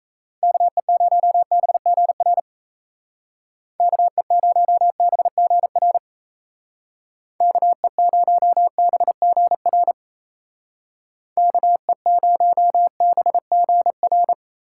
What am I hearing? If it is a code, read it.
Morse code, heard as a keyed tone, about 35 wpm, 700 Hz. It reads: KE0BGR KE0BGR KE0BGR KE0BGR
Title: Speed Racing - USA 2x3 Call Signs 28wpm